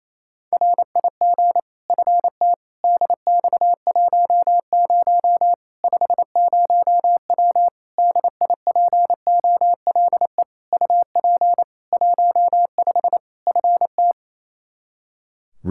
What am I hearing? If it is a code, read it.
Morse code, heard as a keyed tone, about 27 wpm, 700 Hz. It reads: RIG FT DX10 50W DIPOLE UP 15 FT
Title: K1USN SST QSO Practice 28wpm